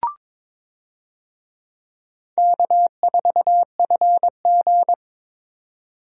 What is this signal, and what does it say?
Morse code, heard as a keyed tone, about 22 wpm, 700 Hz. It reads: K4FG
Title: USA Call Signs - 4 Characters - 22wpm